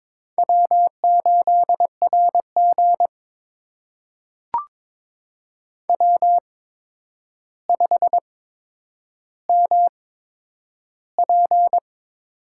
Morse code, heard as a keyed tone, about 22 wpm, spaced as for 9 wpm, 700 Hz. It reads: W8RG W5MP
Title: USA Call Signs - 4 Characters - 8x Character Spacing - 22wpm